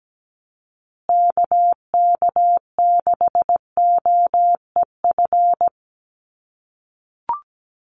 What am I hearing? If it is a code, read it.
Morse code, heard as a keyed tone, about 17 wpm, 700 Hz. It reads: KK6OEF
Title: USA Call Signs Repeated 3x 17wpm